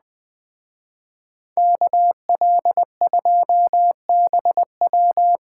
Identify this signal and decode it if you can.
Morse code, heard as a keyed tone, about 21 wpm, 700 Hz. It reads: KL2BW
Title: Speed Racing - USA 2x2 Call Signs 17wpm